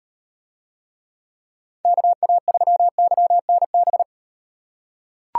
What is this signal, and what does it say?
Morse code, heard as a keyed tone, about 35 wpm, 700 Hz. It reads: KA3YNB